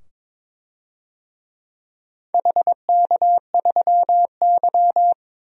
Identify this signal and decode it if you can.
Morse code, heard as a keyed tone, about 22 wpm, 700 Hz. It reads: HK3Y
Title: DXCC-100 Call Signs - 4 Characters - 22wpm